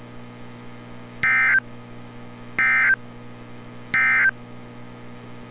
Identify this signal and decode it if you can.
Emergency Alert System (SAME) tones, heard as harsh buzzing bursts. It is heard from 1.1 to 4.4 s.
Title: NOAA Weather Radio - Altoona, PA